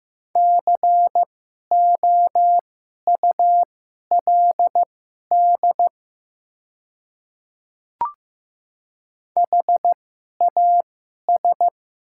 Morse code, heard as a keyed tone, about 15 wpm, spaced as for 11 wpm, 700 Hz. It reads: COULD HAS
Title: 2x Character Spacing - Top 100 Words - 15wpm